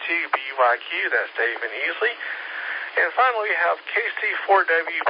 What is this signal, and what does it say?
Emergency Alert System (SAME) tones, heard as harsh buzzing bursts, with speech. It is heard from 0.8 to 4.4 s.